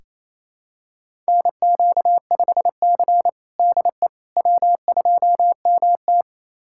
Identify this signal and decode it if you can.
Morse code, heard as a keyed tone, about 28 wpm, 700 Hz. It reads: NQ5C DE W2MT
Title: USA Callsign DE Callsign - 4 Characters - 28wpm